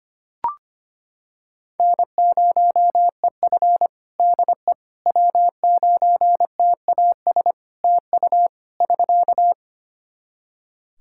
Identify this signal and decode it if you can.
Morse code, heard as a keyed tone, about 25 wpm, 700 Hz. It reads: N0EF DE W9TAS TU <SK>
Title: Practice QSO - Exchange 4 - Ending - 25wpm